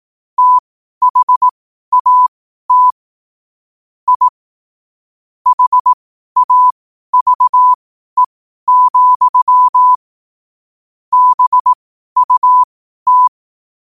Morse code, heard as a keyed tone, about 18 wpm, spaced as for 12 wpm, 1000 Hz. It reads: THAT I HAVE, BUT
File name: qotdfiles / 12wpm